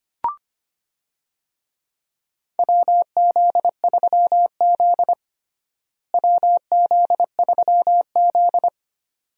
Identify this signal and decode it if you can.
Morse code, heard as a keyed tone, about 25 wpm, 700 Hz. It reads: WZ3Z WZ3Z
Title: USA Call Signs - 4 Characters - Repeated 2x - 25wpm